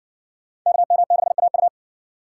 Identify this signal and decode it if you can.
Morse code, heard as a keyed tone, about 59 wpm, 700 Hz. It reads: KM6AU